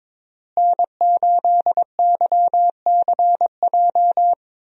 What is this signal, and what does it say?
Morse code, heard as a keyed tone, about 22 wpm, 700 Hz. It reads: N8YCJ
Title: USA Call Signs - 5 Characters - Repeated 3x - 22wpm